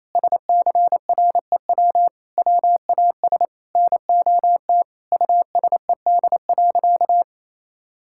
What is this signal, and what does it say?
Morse code, heard as a keyed tone, about 28 wpm, 700 Hz. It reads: SCREW WAS NOT USED.
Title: Sentences from Top 1600 Words 28wpm